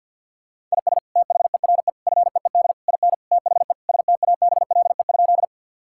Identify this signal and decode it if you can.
Morse code, heard as a keyed tone, about 50 wpm, 700 Hz. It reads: IS THERE FEED IN THE STABLE?
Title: Sentences from Top 1600 Words 50wpm